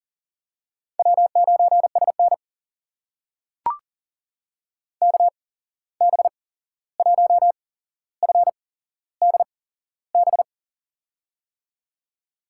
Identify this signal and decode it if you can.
Morse code, heard as a keyed tone, about 40 wpm, spaced as for 15 wpm, 700 Hz. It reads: W9SN KB1FDB
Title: USA Call Signs - 8x Character Spacing - 40wpm